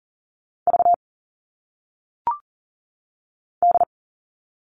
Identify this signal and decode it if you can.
Morse code, heard as a keyed tone, about 39 wpm, 700 Hz. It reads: V D
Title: All Cut Numbers - 1-Digit 40wpm